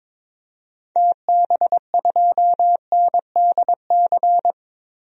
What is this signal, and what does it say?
Morse code, heard as a keyed tone, about 22 wpm, 700 Hz. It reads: TB2NDC
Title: DXCC-100 Call Signs - 6 Characters - 22wpm